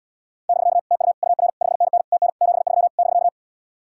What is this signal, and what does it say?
Morse code, heard as a keyed tone, about 78 wpm, 700 Hz. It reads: <BK> TU UR 5NN TN 73 <BK>